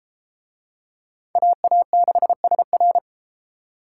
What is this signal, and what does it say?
Morse code, heard as a keyed tone, about 33 wpm, 700 Hz. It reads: AA6SR